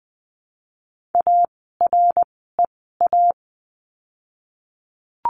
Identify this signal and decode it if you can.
Morse code, heard as a keyed tone, about 20 wpm, spaced as for 15 wpm, 700 Hz. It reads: AREA
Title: 115 Lesson 19f - Letter w - Words 20wpm